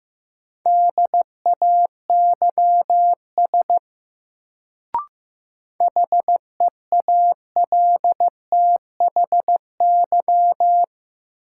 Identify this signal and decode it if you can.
Morse code, heard as a keyed tone, about 15 wpm, 700 Hz. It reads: DAYS HEALTHY
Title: Sentences from Top 1000 Words - Review Part 5 - New Word Drill - 15wpm